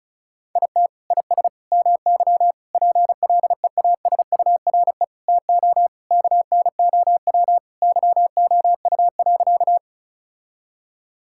Morse code, heard as a keyed tone, about 36 wpm, 700 Hz. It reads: IT IS MY PLEASURE TO KNOW YOU.
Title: Sentences from Top 1600 Words 35wpm